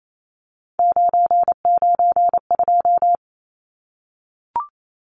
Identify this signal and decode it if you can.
Morse code, heard as a keyed tone, about 28 wpm, 700 Hz. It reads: 992
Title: Numbers - 3-Digit 28wpm